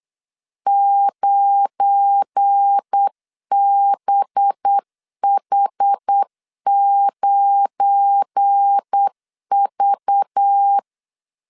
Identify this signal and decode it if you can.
Morse code, heard as a keyed tone, about 8 wpm, 800 Hz. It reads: 9BH9V
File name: LEZIONI / 7 GF9Z / 7 GF9 8wpm 3545